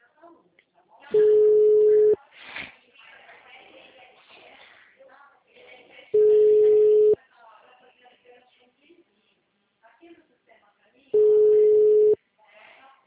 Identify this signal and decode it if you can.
Ringback tone (Europe, 425 Hz), with speech.